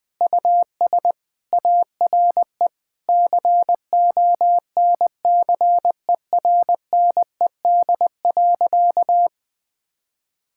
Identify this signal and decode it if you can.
Morse code, heard as a keyed tone, about 21 wpm, 700 Hz. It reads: US ARE CONCERNED.
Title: Sentences from Top 1000 Words - Review Part 6 - 20wpm